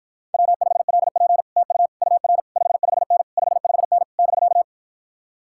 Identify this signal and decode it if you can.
Morse code, heard as a keyed tone, about 53 wpm, 700 Hz. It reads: W5LP TU UR 55N 55N <BK>